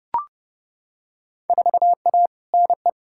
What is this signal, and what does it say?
Morse code, heard as a keyed tone, about 30 wpm, 700 Hz. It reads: 4A NE
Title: Field Day Exchanges - 30wpm